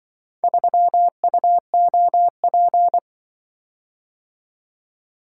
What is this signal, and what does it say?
Morse code, heard as a keyed tone, about 25 wpm, 700 Hz. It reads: 3UOP
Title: Speed-Racing - Four Letter-Number Combinations - 20wpm